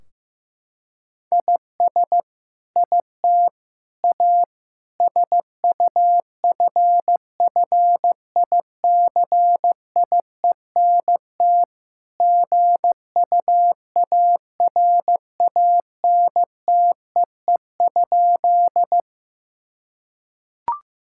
Morse code, heard as a keyed tone, about 15 wpm, 700 Hz. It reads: IS IT A SUFFICIENT GUARANTEE?
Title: Sentences from Top 1600 Words 15wpm